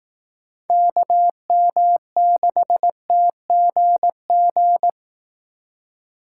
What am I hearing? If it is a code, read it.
Morse code, heard as a keyed tone, about 18 wpm, 700 Hz. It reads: KM6TGG
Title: Speed Racing - USA 2x3 Call Signs 15wpm